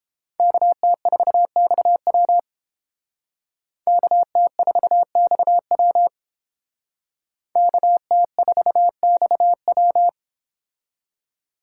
Morse code, heard as a keyed tone, about 30 wpm, 700 Hz. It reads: KT4XW KT4XW KT4XW